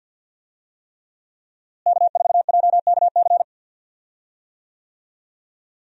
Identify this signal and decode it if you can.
Morse code, heard as a keyed tone, about 47 wpm, 700 Hz. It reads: K4JKC
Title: USA Call Signs - 5 Characters - Repeated 3x - 50wpm